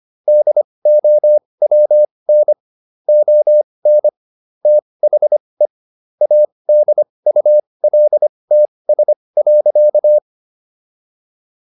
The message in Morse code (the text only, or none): DOWN ON THE ADULTS.